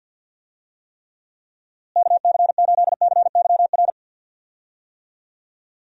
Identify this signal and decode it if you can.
Morse code, heard as a keyed tone, about 51 wpm, 700 Hz. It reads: KC8CYR
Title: USA Call Signs - 6 Characters - Repeated 3x - 50wpm